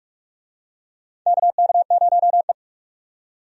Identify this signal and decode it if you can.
Morse code, heard as a keyed tone, about 46 wpm, 700 Hz. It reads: KK0E
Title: USA Call Signs - 4 Characters - 45wpm